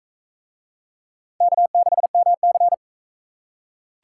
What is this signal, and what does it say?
Morse code, heard as a keyed tone, about 41 wpm, 700 Hz. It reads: K6MC